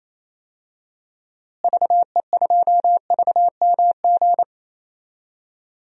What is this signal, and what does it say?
Morse code, heard as a keyed tone, about 28 wpm, 700 Hz. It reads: VE2VMG
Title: DXCC-100 Call Signs - 6 Characters - Repeated 3x - 28wpm